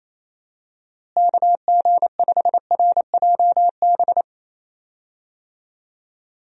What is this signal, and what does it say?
Morse code, heard as a keyed tone, about 28 wpm, 700 Hz. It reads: KG5RJB
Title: USA Call Signs - 6 Characters - Repeated 3x - 28wpm